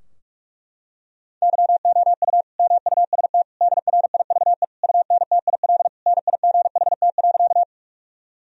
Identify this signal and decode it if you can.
Morse code, heard as a keyed tone, about 42 wpm, 700 Hz. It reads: YOU MUST DRIVE UNTIL NIGHT.
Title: Sentences from Top 1000 Words - Review Part 2 - 45wpm